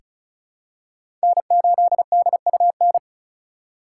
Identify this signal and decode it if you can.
Morse code, heard as a keyed tone, about 34 wpm, 700 Hz. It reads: N8DUN
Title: USA Call Signs - 5 Characters - 4x Character Spacing - 35wpm